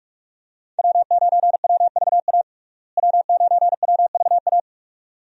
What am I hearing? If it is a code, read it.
Morse code, heard as a keyed tone, about 46 wpm, 700 Hz. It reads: W9WVA W9WVA
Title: USA Call Signs - 5 Characters - Repeated 2x - 45wpm